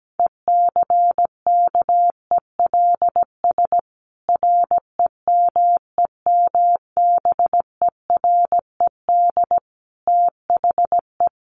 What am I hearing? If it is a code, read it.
Morse code, heard as a keyed tone, about 17 wpm, 700 Hz. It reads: ECKELS REMEMBERED THE
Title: A Sound Of Thunder 17wpm